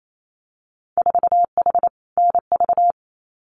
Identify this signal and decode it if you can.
Morse code, heard as a keyed tone, about 28 wpm, 700 Hz. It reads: 4H NV
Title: Winter Field Day Exchanges Encore 28wpm